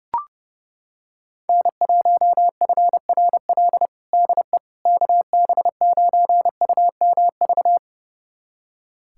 Morse code, heard as a keyed tone, about 30 wpm, 690 Hz. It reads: N1FRL DE KB9UMV
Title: USA Callsign DE Callsign 30wpm